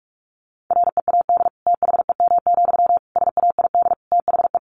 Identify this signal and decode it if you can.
Morse code, heard as a keyed tone, about 44 wpm, 700 Hz. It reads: READ THEM, SAID THE
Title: Sentences from Top 400 Words 45wpm